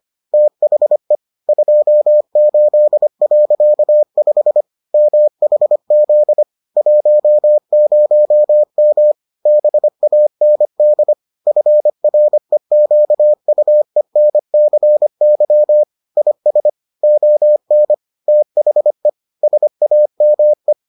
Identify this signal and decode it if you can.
Morse code, heard as a keyed tone, about 25 wpm, 600 Hz. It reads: THE 28.5 MHZ 10M BAND FREQUENCY IS ON THE SAME